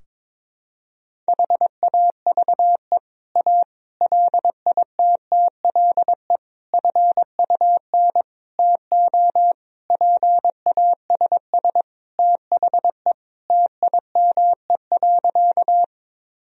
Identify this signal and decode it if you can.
Morse code, heard as a keyed tone, about 22 wpm, 700 Hz. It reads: HAVE A LITTLE FUN TO PASS THE TIME.
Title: Sentences from Top 1000 Words - Review Part 4 - 22wpm